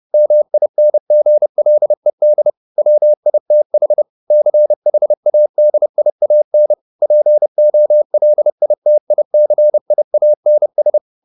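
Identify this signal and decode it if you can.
Morse code, heard as a keyed tone, about 30 wpm, 600 Hz. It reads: MINGLED WITH CHADIAN POLITICIANS
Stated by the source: Morse code at 30 wpm, the novel Never by Ken Follett in English, from CW med Gnister.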